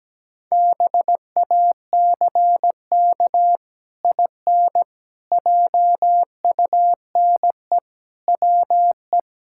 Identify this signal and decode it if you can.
Morse code, heard as a keyed tone, about 17 wpm, 700 Hz. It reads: BACK IN JUNE WE
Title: Pangram Challenge v2 - 17wpm